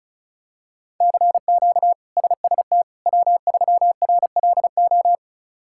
Morse code, heard as a keyed tone, about 34 wpm, 700 Hz. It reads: CQ SST W3RLO
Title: K1USN SST QSO Practice - 35wpm